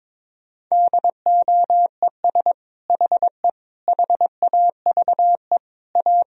Morse code, heard as a keyed tone, about 22 wpm, 700 Hz. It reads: DOES HE HAVE A
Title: Sentences from Top 1000 Words - Review Part 4 - 22wpm